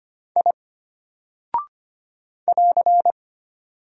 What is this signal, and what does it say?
Morse code, heard as a keyed tone, about 25 wpm, 700 Hz. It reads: I <AR>
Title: Single Character (Letter:Number:Punctuation:Prosign) Alt 25wpm